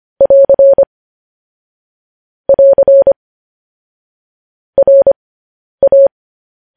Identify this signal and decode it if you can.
Morse code, heard as a keyed tone, about 25 wpm, spaced as for 10 wpm, 550 Hz. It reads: <AR> <AR> RA
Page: news headlines in morse code at 10 wpm